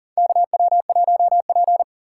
Morse code, heard as a keyed tone, about 40 wpm, 700 Hz. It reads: KW1P
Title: USA Call Signs - 4 Characters - 6x Character Spacing - 40wpm